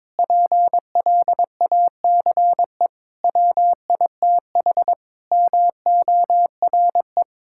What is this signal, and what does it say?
Morse code, heard as a keyed tone, about 21 wpm, 700 Hz. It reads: PLACE WITH MORE
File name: Sentences with 7-Word Phrases - 22wpm